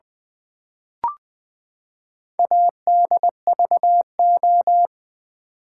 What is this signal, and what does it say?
Morse code, heard as a keyed tone, about 20 wpm, 700 Hz. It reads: ADVO
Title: Build A Word - 4 Letters At A Time - 8-Letter Words - 20wpm